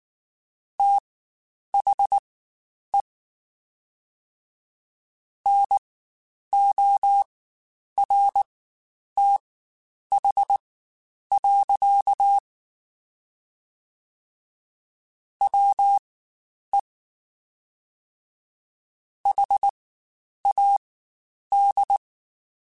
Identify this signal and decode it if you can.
Morse code, heard as a keyed tone, about 19 wpm, spaced as for 9 wpm, 800 Hz. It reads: THE NORTH. WE HAD
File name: Spaced Characters / Spaced Characters-mp3s / Weather 4 24 sp @18